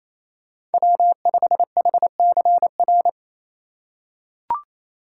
Morse code, heard as a keyed tone, about 28 wpm, 700 Hz. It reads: W5HCR
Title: USA Call Signs - 6x Character Spacing - 28wpm